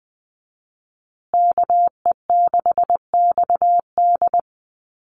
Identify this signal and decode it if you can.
Morse code, heard as a keyed tone, about 20 wpm, 700 Hz. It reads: KE6XD
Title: USA Call Signs Repeated 3x 20wpm